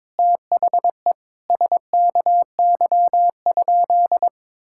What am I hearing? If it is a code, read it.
Morse code, heard as a keyed tone, about 22 wpm, 700 Hz. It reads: THE SKY?
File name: practice / 22wpm / Sentences with 4-Word Phrases - 22wpm